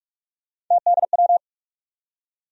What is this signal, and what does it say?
Morse code, heard as a keyed tone, about 42 wpm, 700 Hz. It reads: TDW
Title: USA Call Sign Postfixes - 3 Letters 45wpm